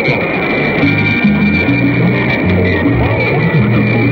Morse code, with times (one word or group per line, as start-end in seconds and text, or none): none